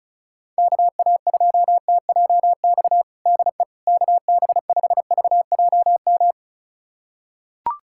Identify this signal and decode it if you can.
Morse code, heard as a keyed tone, about 36 wpm, 700 Hz. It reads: KA2TJX DE KB5VJM
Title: USA Callsign DE Callsign 35wpm